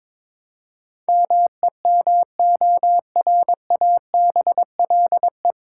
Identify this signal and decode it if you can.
Morse code, heard as a keyed tone, about 22 wpm, 700 Hz. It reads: MEMORABLE
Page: Morse Code Ninja practice 22wpm